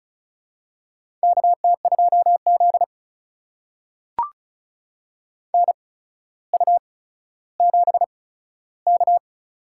Morse code, heard as a keyed tone, about 36 wpm, spaced as for 15 wpm, 700 Hz. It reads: KT2Z NU7K
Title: USA Call Signs - 4 Characters - 8x Character Spacing - 35wpm